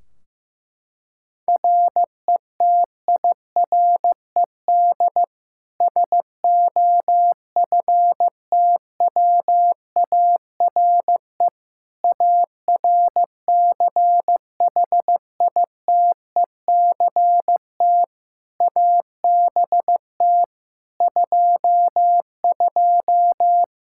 Morse code, heard as a keyed tone, about 15 wpm, 700 Hz. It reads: RETIRED SOFTWARE ARCHITECT ABT 22
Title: Practice QSO - Exchange 3 - More Info - 15wpm